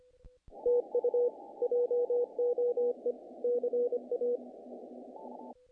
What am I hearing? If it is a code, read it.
Morse code, heard as a keyed tone, about 25 wpm, 490 Hz. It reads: TU JOE CA
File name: contests / naqp-cw-k5tr-op-k5pi- -1800